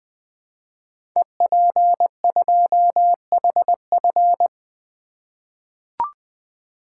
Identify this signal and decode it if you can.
Morse code, heard as a keyed tone, about 20 wpm, 700 Hz. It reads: EP2HF
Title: Code Groups - 5 Letter:Numbers - 20wpm